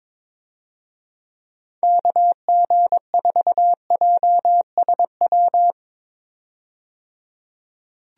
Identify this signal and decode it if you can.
Morse code, heard as a keyed tone, about 22 wpm, 700 Hz. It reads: KG4JSW
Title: USA Call Signs - 6 Characters - 22wpm